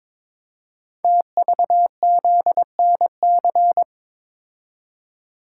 Morse code, heard as a keyed tone, about 22 wpm, 690 Hz. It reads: TVZNC
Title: Code Groups - 5 Letter:Numbers - 22wpm